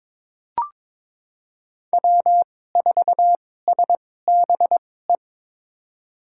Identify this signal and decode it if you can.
Morse code, heard as a keyed tone, about 21 wpm, spaced as for 16 wpm, 700 Hz. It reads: W4SBE
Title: USA Call Signs - 5 Characters - 2x Character Spacing - 22wpm